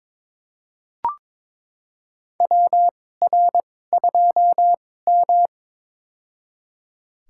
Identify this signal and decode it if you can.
Morse code, heard as a keyed tone, about 21 wpm, spaced as for 16 wpm, 700 Hz. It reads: WR2M
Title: USA Call Signs - 4 Characters - 2x Character Spacing - 22wpm